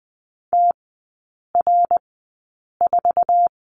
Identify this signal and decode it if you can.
Morse code, heard as a keyed tone, about 20 wpm, spaced as for 8 wpm, 690 Hz. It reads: TR4
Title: 060 Lesson 11e - Letter h - Three Characters 20wpm